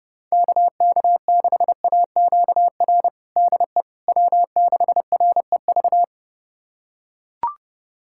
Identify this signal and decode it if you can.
Morse code, heard as a keyed tone, about 31 wpm, 700 Hz. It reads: KK6AQR DE W6REV
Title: USA Callsign DE Callsign 30wpm